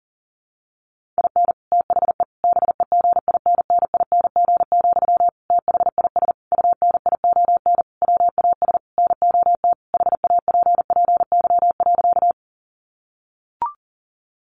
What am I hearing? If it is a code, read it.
Morse code, heard as a keyed tone, about 37 wpm, 700 Hz. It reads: IN THE BEGINNING, THIS UNION WAS NOT HAPPY.
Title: Sentences from Top 1400 Words 40wpm